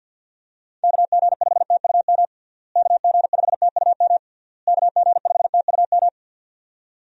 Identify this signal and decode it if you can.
Morse code, heard as a keyed tone, about 51 wpm, 700 Hz. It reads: KG5TUM KG5TUM KG5TUM
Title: USA Call Signs - 6 Characters - Repeated 3x - 50wpm